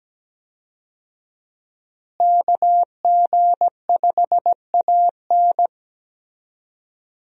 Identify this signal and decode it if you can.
Morse code, heard as a keyed tone, about 17 wpm, 700 Hz. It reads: KG5AN
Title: USA Call Signs - 5 Characters - Repeated 2x - 17wpm